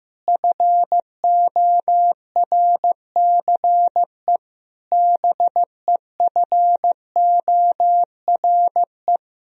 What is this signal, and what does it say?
Morse code, heard as a keyed tone, about 15 wpm, 700 Hz. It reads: FORCE BEFORE
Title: Sentences from Top 1000 Words - Review Part 3 - 15wpm